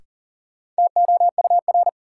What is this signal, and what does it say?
Morse code, heard as a keyed tone, about 40 wpm, 700 Hz. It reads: TOUR
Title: Sentences from Top 1400 Words - New Word Drill Alt - 40wpm